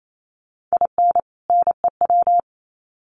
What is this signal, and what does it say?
Morse code, heard as a keyed tone, about 28 wpm, 700 Hz. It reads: IN NEW
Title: Sets of 2 Words 28wpm